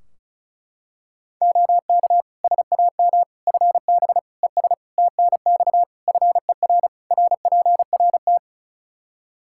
Morse code, heard as a keyed tone, about 36 wpm, 700 Hz. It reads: OK SAM FB ES TNX FER RPRT
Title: Practice QSO - Exchange 2 - Info - 35wpm